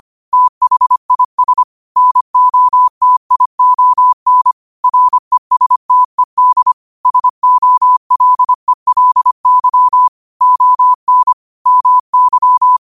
Morse code, heard as a keyed tone, about 25 wpm, 1000 Hz. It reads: THIS NOTION RESTED SOLELY ON MY
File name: qotdfiles / 25wpm